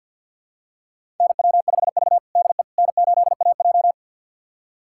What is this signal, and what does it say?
Morse code, heard as a keyed tone, about 52 wpm, 700 Hz. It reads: NW5V DE N8AJ
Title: USA Callsign DE Callsign - 4 Characters - 50wpm